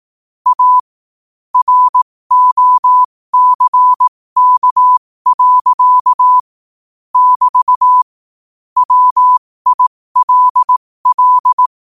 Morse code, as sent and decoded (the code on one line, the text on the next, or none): .- / .-. --- -.-. -.- .-.-.- / -...- / .-- .. .-.. .-..
A ROCK. <BT> WILL